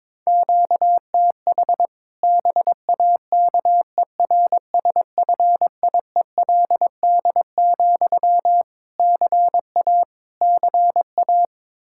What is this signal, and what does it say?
Morse code, heard as a keyed tone, about 22 wpm, 700 Hz. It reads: QTH BAKERSFIELD, CA CA
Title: Practice QSO - Exchange 1 - Beginning - 22wpm